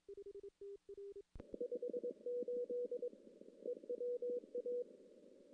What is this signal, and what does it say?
Morse code, heard as a keyed tone, about 22 wpm, 490 Hz. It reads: 58 EWA
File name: CW / CW-SS-k5tr- -1700